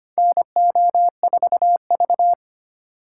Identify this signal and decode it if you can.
Morse code, heard as a keyed tone, about 25 wpm, 700 Hz. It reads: NO4V
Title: USA Call Signs - 2x Character Spacing - 25wpm